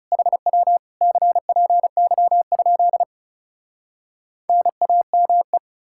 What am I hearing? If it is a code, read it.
Morse code, heard as a keyed tone, about 40 wpm, 700 Hz. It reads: HW CPY? NAME